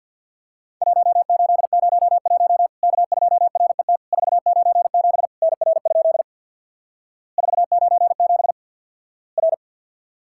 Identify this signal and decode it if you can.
Morse code, heard as a keyed tone, about 50 wpm, 700 Hz. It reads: 1801 K2RET 497 NR? 497 R
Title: Worked All Europe DX Contest - QTC Exchange with Repeats - 50wpm